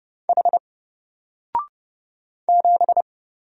Morse code, heard as a keyed tone, about 30 wpm, 700 Hz. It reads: H 7
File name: Single-Letter-Number-30wpm